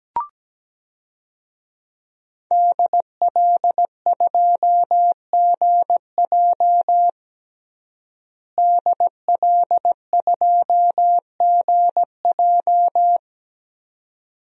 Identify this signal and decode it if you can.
Morse code, heard as a keyed tone, about 17 wpm, 700 Hz. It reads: DL2GJ DL2GJ
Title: DXCC-100 Call Signs - 5 Characters - Repeated 2x - 17wpm